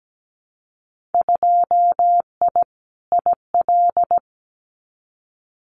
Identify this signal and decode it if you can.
Morse code, heard as a keyed tone, about 17 wpm, 700 Hz. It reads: 2I IL
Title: Winter Field Day Exchanges 17wpm